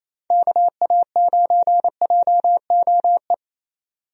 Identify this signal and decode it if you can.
Morse code, heard as a keyed tone, about 28 wpm, 700 Hz. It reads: KA9JOE
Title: USA Call Signs - 6 Characters - 28wpm